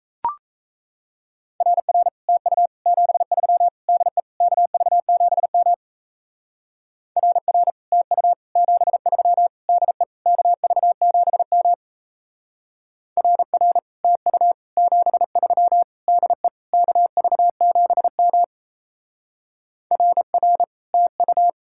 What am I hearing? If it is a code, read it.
Morse code, heard as a keyed tone, about 36 wpm, 700 Hz. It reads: RR TU 73 DE KV7M RR TU 73 DE KV7M RR TU 73 DE KV7M RR TU